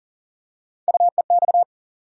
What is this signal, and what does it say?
Morse code, heard as a keyed tone, about 39 wpm, 700 Hz. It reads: UEX
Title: USA Call Sign Postfixes - 3 Letters 40wpm